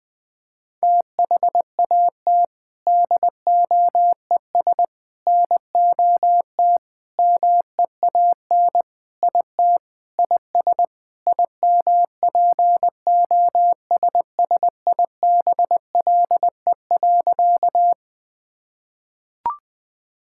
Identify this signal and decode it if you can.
Morse code, heard as a keyed tone, about 20 wpm, 700 Hz. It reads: THAT DOES NOT MEAN IT IS IMPOSSIBLE.
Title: Sentences with 4-Word Phrases - Spoken Beginning - 20wpm